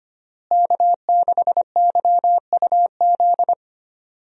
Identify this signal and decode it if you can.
Morse code, heard as a keyed tone, about 25 wpm, 700 Hz. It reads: K6YUZ
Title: USA Call Signs - 5 Characters - Repeated 2x - 25wpm